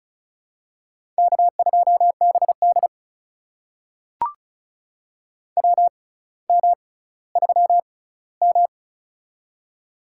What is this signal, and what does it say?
Morse code, heard as a keyed tone, about 36 wpm, spaced as for 18 wpm, 700 Hz. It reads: K2BD WM3M
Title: USA Call Signs - 4 Characters - 6x Character Spacing - 35wpm